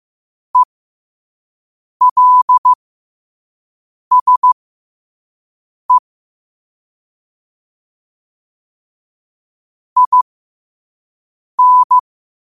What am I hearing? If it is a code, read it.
Morse code, heard as a keyed tone, about 15 wpm, spaced as for 5 wpm, 1000 Hz. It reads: ELSE IN
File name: qotdfiles / 5wpm